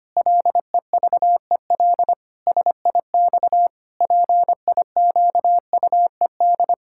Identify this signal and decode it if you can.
Morse code, heard as a keyed tone, about 25 wpm, 700 Hz. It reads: LEVEL SIX PIQUED
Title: Pangram Challenge v2 - 25wpm